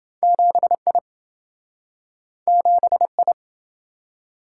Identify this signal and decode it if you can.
Morse code, heard as a keyed tone, about 26 wpm, 700 Hz. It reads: 7I 7I